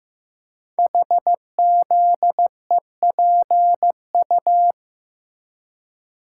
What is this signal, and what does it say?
Morse code, heard as a keyed tone, about 15 wpm, 700 Hz. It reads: HZEPU
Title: Code Groups - 5 Letters - 15wpm